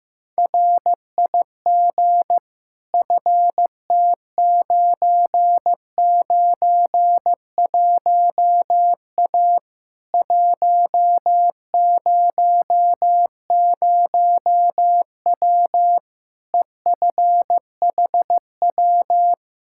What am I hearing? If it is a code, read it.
Morse code, heard as a keyed tone, about 15 wpm, 700 Hz. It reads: RIG FT991A 100W EFHW
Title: Practice QSO - Exchange 2 - Info - 15wpm